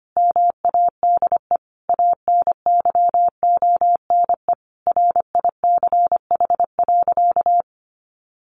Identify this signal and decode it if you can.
Morse code, heard as a keyed tone, about 25 wpm, 700 Hz. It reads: MADE ANYONE RICH.
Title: Sentences from Top 1100 Words 25wpm